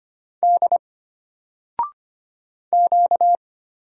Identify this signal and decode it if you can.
Morse code, heard as a keyed tone, about 25 wpm, 700 Hz. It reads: D Q
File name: Single-Letter-Number-25wpm